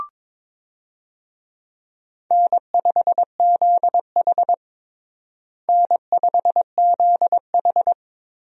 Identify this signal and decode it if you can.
Morse code, heard as a keyed tone, about 22 wpm, 700 Hz. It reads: N5ZH N5ZH
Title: USA Call Signs - 4 Characters - Repeated 2x - 22wpm